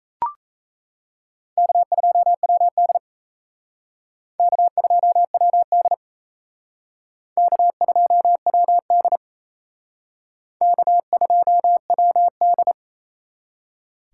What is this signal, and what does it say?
Morse code, heard as a keyed tone, about 34 wpm, 700 Hz. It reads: K2WD K2WD K2WD K2WD
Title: Speed Racing - USA 1x2 Call Signs 28wpm